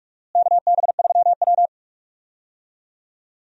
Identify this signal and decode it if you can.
Morse code, heard as a keyed tone, about 45 wpm, 700 Hz. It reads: KB3W
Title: USA Call Signs - 4 Characters - Repeated 2x - 45wpm